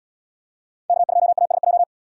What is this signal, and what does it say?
Morse code, heard as a keyed tone, about 77 wpm, 700 Hz. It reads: K2ASJ